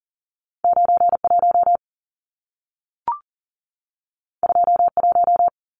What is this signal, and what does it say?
Morse code, heard as a keyed tone, about 40 wpm, 700 Hz. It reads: 91 21
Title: Numbers - 2-Digit 40wpm